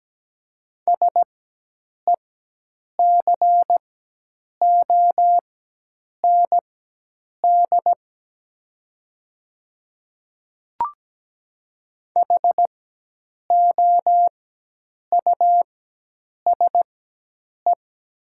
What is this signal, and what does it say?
Morse code, heard as a keyed tone, about 17 wpm, spaced as for 8 wpm, 700 Hz. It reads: SECOND HOUSE
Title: 4x Character Spacing - Top 300 Words - 17wpm